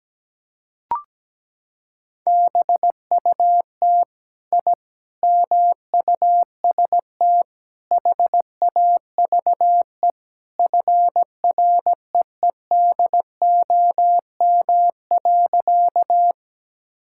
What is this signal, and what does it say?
Morse code, heard as a keyed tone, about 17 wpm, 700 Hz. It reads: BUT I MUST HAVE FREEDOM.
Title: Sentences from Top 1000 Words - Review Part 5 - 17wpm